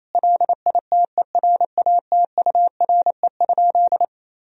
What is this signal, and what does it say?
Morse code, heard as a keyed tone, about 27 wpm, 700 Hz. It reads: LITERATURE?
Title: Sentences from Top 1000 Words - Review Part 3 - 28wpm